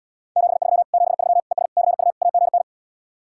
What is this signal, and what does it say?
Morse code, heard as a keyed tone, about 77 wpm, 700 Hz. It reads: 73 73 ES CU AGN